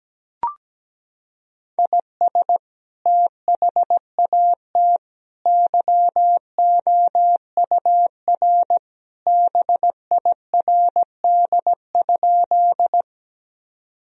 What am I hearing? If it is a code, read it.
Morse code, heard as a keyed tone, about 17 wpm, 700 Hz. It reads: IS THAT YOUR BIRD?
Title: Sentences from Top 1000 Words - Review Part 2 - 17wpm